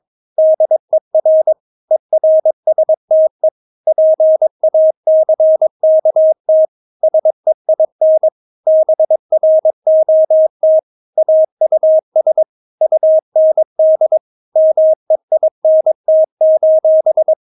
Morse code, heard as a keyed tone, about 22 wpm, 630 Hz. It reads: DER ERSTE PACKT SEIN BROT AUS UND MEINT:
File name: Witz3 22wpm